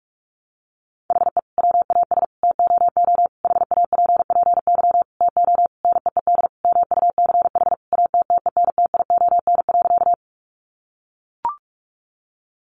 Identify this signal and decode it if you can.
Morse code, heard as a keyed tone, about 44 wpm, 700 Hz. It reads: HE WAS TOO HAPPY TO NEED MUCH ATTENTION.
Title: Sentences from Top 1000 Words 45wpm